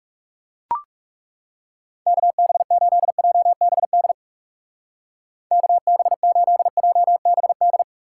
Morse code, heard as a keyed tone, about 40 wpm, 700 Hz. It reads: KB8JBD KB8JBD